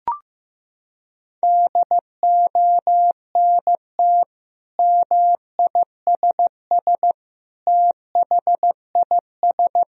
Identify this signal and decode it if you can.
Morse code, heard as a keyed tone, about 15 wpm, 700 Hz. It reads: DONT MISS THIS
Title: Sentences from Top 1000 Words - Review Part 3 - 15wpm